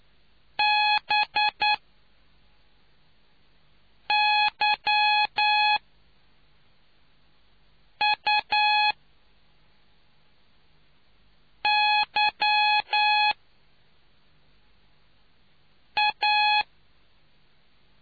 Morse code, heard as a keyed tone, about 9 wpm, spaced as for 3 wpm, 840 Hz. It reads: BYUYA